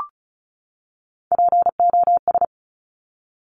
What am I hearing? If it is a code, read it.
Morse code, heard as a keyed tone, about 36 wpm, 690 Hz. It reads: POS
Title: Top Three Letter Combinations in Words 35wpm